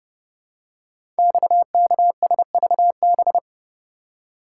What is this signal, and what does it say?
Morse code, heard as a keyed tone, about 30 wpm, 690 Hz. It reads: XKSVB
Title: Code Groups - 5 Letters - 30wpm